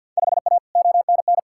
Morse code, heard as a keyed tone, about 52 wpm, 700 Hz. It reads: 5A ONN
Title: Field Day Exchanges Encore - 50wpm